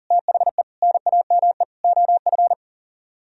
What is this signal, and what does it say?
Morse code, heard as a keyed tone, about 51 wpm, 700 Hz. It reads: THE NAME OF